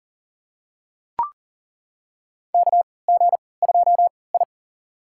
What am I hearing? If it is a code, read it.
Morse code, heard as a keyed tone, about 40 wpm, spaced as for 23 wpm, 700 Hz. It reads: KG2I
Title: USA Call Signs - 4 Characters - 3x Character Spacing - 40wpm